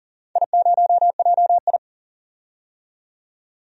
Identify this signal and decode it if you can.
Morse code, heard as a keyed tone, about 40 wpm, 700 Hz. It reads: I0JI
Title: DXCC-100 Call Signs - 4 Characters - 40wpm